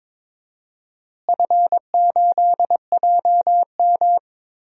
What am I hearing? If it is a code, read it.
Morse code, heard as a keyed tone, about 23 wpm, 700 Hz. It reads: F8JM
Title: DXCC-100 Call Signs - 4 Characters - Repeated 2x - 22wpm